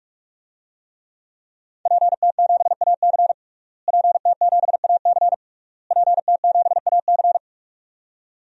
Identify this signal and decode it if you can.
Morse code, heard as a keyed tone, about 46 wpm, 700 Hz. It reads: PT7AC PT7AC PT7AC
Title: DXCC-100 Call Signs - 5 Characters - Repeated 3x - 45wpm